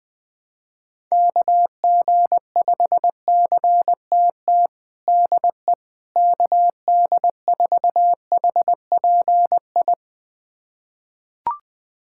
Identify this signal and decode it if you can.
Morse code, heard as a keyed tone, about 20 wpm, 700 Hz. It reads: KG5CTT DE KD4HPI
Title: USA Callsign DE Callsign 20wpm